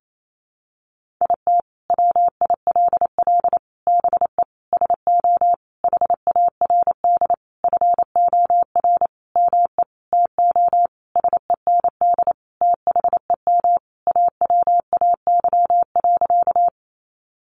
Morse code, heard as a keyed tone, about 28 wpm, 700 Hz. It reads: IT WILL BE SO HARD FOR ME TO SEND THEM AWAY.
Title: Sentences from Top 1500 Words 28wpm